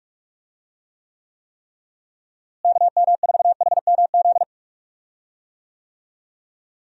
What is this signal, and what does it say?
Morse code, heard as a keyed tone, about 44 wpm, 700 Hz. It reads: KM4HMZ
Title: USA Call Signs - 6 Characters - Repeated 3x - 45wpm